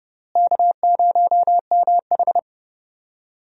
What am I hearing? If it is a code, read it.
Morse code, heard as a keyed tone, about 30 wpm, 700 Hz. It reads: K0MH
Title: USA Call Signs - 4 Characters - Repeated 3x - 30wpm